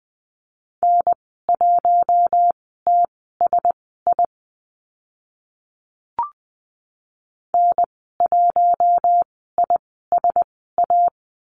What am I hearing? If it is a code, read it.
Morse code, heard as a keyed tone, about 20 wpm, spaced as for 14 wpm, 700 Hz. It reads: N1TSI N1ISA
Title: 042 Lesson 8g - Number 1 - Call Signs 20wpm